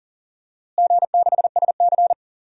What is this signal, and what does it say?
Morse code, heard as a keyed tone, about 40 wpm, 700 Hz. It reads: G6SC
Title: DXCC-100 Call Signs - 4 Characters - Repeated 3x - 40wpm